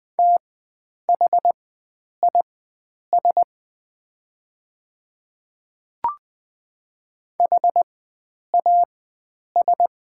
Morse code, heard as a keyed tone, about 20 wpm, spaced as for 9 wpm, 700 Hz. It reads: THIS HAS
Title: 4x Character Spacing - Top 100 Words - 20wpm